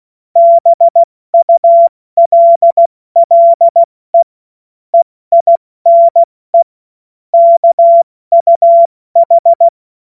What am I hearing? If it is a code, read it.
Morse code, heard as a keyed tone, about 16 wpm, spaced as for 14 wpm, 680 Hz. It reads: BULLE EINE KUH
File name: Morsekurs / Witze / Witz 13. April 2021, 70 F80